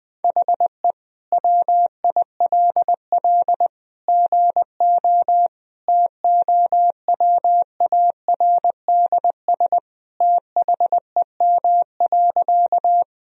Morse code, heard as a keyed tone, about 20 wpm, 700 Hz. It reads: HE WILL GO TOWARDS THEM.
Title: Sentences from Top 1000 Words - Review Part 2 - 20wpm